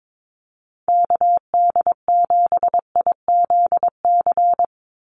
Morse code, heard as a keyed tone, about 22 wpm, 700 Hz. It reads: KD7IZC
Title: USA Call Signs 22wpm